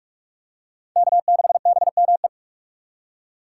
Morse code, heard as a keyed tone, about 46 wpm, 700 Hz. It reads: K6BME
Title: USA Call Signs - 3x Character Spacing - 45wpm